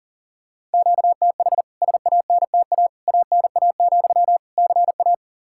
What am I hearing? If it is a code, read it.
Morse code, heard as a keyed tone, about 40 wpm, 700 Hz. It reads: QTH SANTA ANA, CA
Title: K1USN SST QSO Practice 40wpm